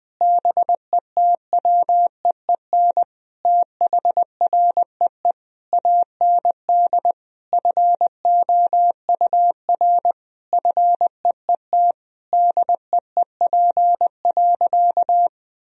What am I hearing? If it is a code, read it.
Morse code, heard as a keyed tone, about 20 wpm, 700 Hz. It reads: BETWEEN THREE AND FOUR FEET DEEP.
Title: Sentences from Top 1000 Words - Review Part 3 - 20wpm